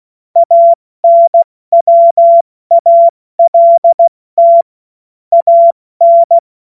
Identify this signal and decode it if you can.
Morse code, heard as a keyed tone, about 16 wpm, 680 Hz. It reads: ANWALT AN
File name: Morsekurs / Witze / Witz 06. September 2022 70 F80